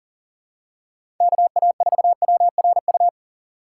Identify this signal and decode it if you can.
Morse code, heard as a keyed tone, about 40 wpm, 700 Hz. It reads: KA4WRU
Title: USA Call Signs - 6 Characters - 2x Character Spacing - 40wpm